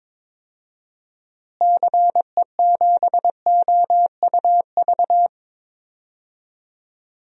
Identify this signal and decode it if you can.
Morse code, heard as a keyed tone, about 23 wpm, 700 Hz. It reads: CE7OUV
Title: DXCC-100 Call Signs - 6 Characters - Repeated 3x - 22wpm